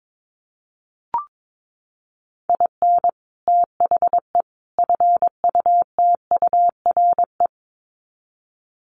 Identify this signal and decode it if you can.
Morse code, heard as a keyed tone, about 22 wpm, 700 Hz. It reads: IN THE FUTURE
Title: Sets of 3 Words 22wpm